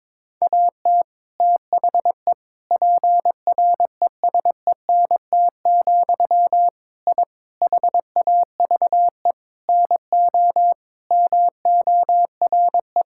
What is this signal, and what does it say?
Morse code, heard as a keyed tone, about 22 wpm, 700 Hz. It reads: AT THE PRESENT, I HAVE NO MORE
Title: Sentences from Top 1000 Words - Review Part 5 - 22wpm